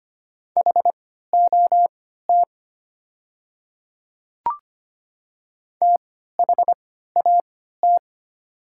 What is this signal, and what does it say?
Morse code, heard as a keyed tone, about 25 wpm, spaced as for 14 wpm, 700 Hz. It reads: HOT THAT
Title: 3x Character Spacing - Top 200 Words - 25wpm